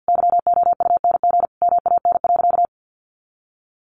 Morse code, heard as a keyed tone, about 50 wpm, 700 Hz. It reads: YOUNG MAN.
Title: Sentences from Top 500 Words 50wpm